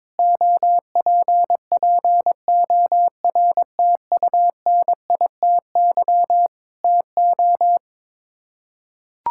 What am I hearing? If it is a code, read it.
Morse code, heard as a keyed tone, about 22 wpm, 700 Hz. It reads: OPPORTUNITY TO
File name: practice / 22wpm / Sets of 7 Words - 22wpm